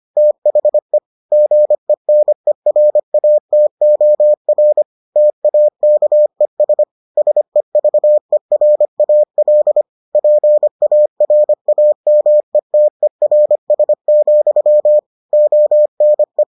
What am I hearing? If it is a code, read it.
Morse code, heard as a keyed tone, about 25 wpm, 600 Hz. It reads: THE GENERATOR TAKES SEVERAL PARAMETERS, ONE